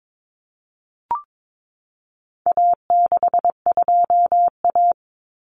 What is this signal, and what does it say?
Morse code, heard as a keyed tone, about 22 wpm, 700 Hz. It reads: A62A
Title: Common Cut Numbers - 4-Digit 22wpm